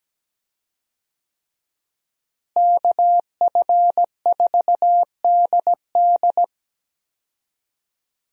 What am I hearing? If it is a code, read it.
Morse code, heard as a keyed tone, about 17 wpm, 700 Hz. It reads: KF4DD
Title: USA Call Signs - 5 Characters - Repeated 3x - 17wpm